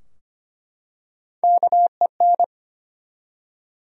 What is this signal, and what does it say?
Morse code, heard as a keyed tone, about 25 wpm, 700 Hz. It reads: KEN
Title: USA Common Names - Max 3 Letters - 25wpm